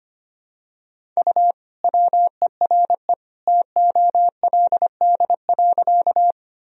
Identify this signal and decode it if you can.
Morse code, heard as a keyed tone, about 25 wpm, 700 Hz. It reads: U WERE TOLD.
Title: Sentences from Top 400 Words and Abbreviations 25wpm